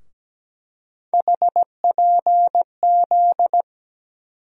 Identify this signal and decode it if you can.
Morse code, heard as a keyed tone, about 17 wpm, 700 Hz. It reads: HPZ
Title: USA Call Sign Postfixes - 3 Letters 17wpm